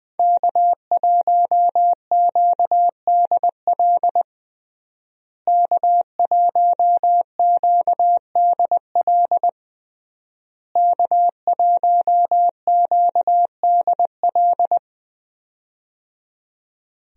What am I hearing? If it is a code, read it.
Morse code, heard as a keyed tone, about 20 wpm, 700 Hz. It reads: K1QDL K1QDL K1QDL
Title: USA Call Signs - 5 Characters - Repeated 3x - 20wpm